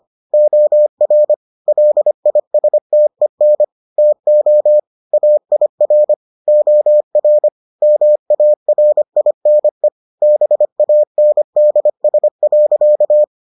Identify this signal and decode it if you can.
Morse code, heard as a keyed tone, about 25 wpm, 600 Hz. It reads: OR LISTEN TO AIR OR MARINE BANDS.